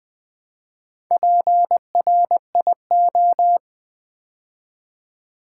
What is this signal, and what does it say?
Morse code, heard as a keyed tone, about 20 wpm, 700 Hz. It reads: PRIO
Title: Build A Word - 4 Letters At A Time - 8-Letter Words - 20wpm